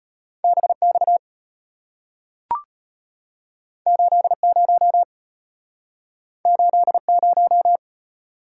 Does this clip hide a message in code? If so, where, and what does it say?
Yes: Morse code from 0.4 to 7.8 s, heard as a keyed tone, about 34 wpm, 700 Hz. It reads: BX 80 80